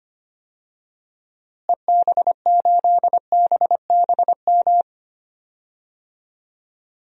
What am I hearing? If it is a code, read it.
Morse code, heard as a keyed tone, about 25 wpm, 700 Hz. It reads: EB8BBM
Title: DXCC-100 Call Signs - 6 Characters - Repeated 2x - 25wpm